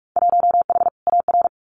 Morse code, heard as a keyed tone, about 46 wpm, 700 Hz. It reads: 1H AR
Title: Winter Field Day Exchanges Encore 45wpm